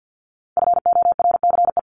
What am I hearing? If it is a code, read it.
Morse code, heard as a keyed tone, about 50 wpm, 700 Hz. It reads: FORCE
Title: ES:S Suffix - Top 100 Words with Base Words 50wpm